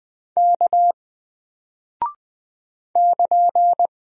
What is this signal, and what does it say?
Morse code, heard as a keyed tone, about 20 wpm, 700 Hz. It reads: K <KN>
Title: Single Character (Letter:Number:Punctuation:Prosign) Alt 20wpm